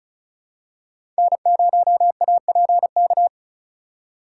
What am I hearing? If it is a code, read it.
Morse code, heard as a keyed tone, about 36 wpm, 700 Hz. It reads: N0APK
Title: USA Call Signs - 5 Characters - 2x Character Spacing - 35wpm